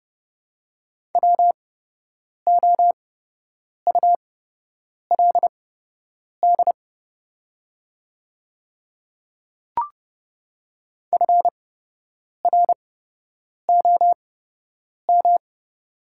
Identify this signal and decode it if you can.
Morse code, heard as a keyed tone, about 30 wpm, spaced as for 8 wpm, 700 Hz. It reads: WOULD FROM
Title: 8x Character Spacing - Top 100 Words - 30wpm